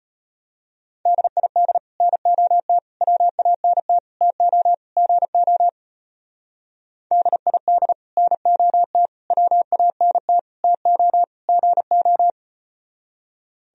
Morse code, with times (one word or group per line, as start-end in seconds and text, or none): 1.1-1.8 DID
2.0-2.8 NOT
3.0-4.0 WANT
4.2-4.7 TO
5.0-5.7 GO
7.1-7.9 DID
8.2-9.1 NOT
9.3-10.4 WANT
10.6-11.2 TO
11.5-12.3 GO